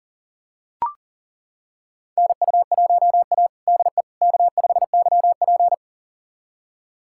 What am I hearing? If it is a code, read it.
Morse code, heard as a keyed tone, about 40 wpm, 700 Hz. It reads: NU1A DE K5YP
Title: USA Callsign DE Callsign - 4 Characters - 40wpm